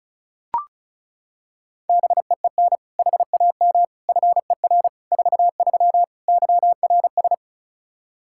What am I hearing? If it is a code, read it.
Morse code, heard as a keyed tone, about 34 wpm, 700 Hz. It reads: BEEN HAM FER 43 YRS
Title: Practice QSO - Exchange 3 - More Info - 35wpm